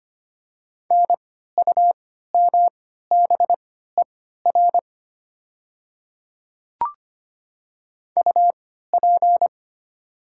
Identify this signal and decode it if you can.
Morse code, heard as a keyed tone, about 25 wpm, spaced as for 14 wpm, 700 Hz. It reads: NUMBER UP
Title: 3x Character Spacing - Top 100 Words - 25wpm